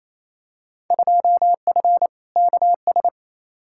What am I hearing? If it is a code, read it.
Morse code, heard as a keyed tone, about 28 wpm, 700 Hz. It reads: 2F KS
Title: Field Day Exchanges Encore - 28wpm